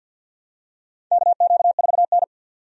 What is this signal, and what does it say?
Morse code, heard as a keyed tone, about 53 wpm, 700 Hz. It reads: KQ4N
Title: USA Call Signs - 4 Characters - 50wpm